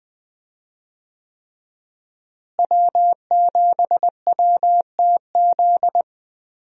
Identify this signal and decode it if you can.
Morse code, heard as a keyed tone, about 21 wpm, 700 Hz. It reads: W7WTZ
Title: USA Call Signs - 5 Characters - Repeated 3x - 20wpm